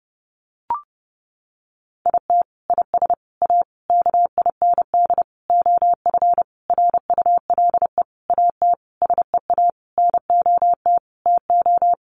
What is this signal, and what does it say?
Morse code, heard as a keyed tone, about 30 wpm, 700 Hz. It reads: IT IS A KIND OF RULE AT SEA NOT TO
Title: Sentences from Top 1500 Words 30wpm